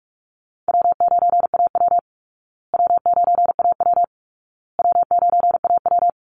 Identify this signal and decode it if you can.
Morse code, heard as a keyed tone, about 44 wpm, 700 Hz. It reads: W9AW W9AW W9AW
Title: Easy Call Signs Repeated 3x - 45wpm